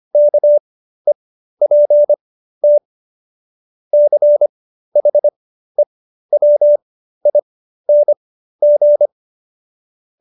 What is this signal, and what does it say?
Morse code, heard as a keyed tone, about 25 wpm, spaced as for 13 wpm, 590 Hz. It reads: KEPT CHEWING